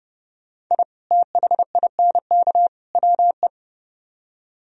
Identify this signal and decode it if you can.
Morse code, heard as a keyed tone, about 30 wpm, 700 Hz. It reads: I THINK WE